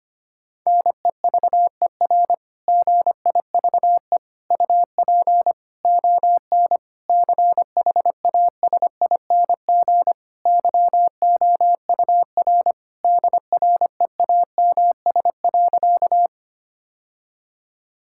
Morse code, heard as a keyed tone, about 25 wpm, 700 Hz. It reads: NEVER GIVE UP ON CHASING YOUR DREAMS.
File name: practice / 25wpm / Difficult Sentences with Hints - 25wpm